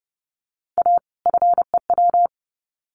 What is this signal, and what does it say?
Morse code, heard as a keyed tone, about 30 wpm, 700 Hz. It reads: A FEW
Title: Sets of 2 Words 30wpm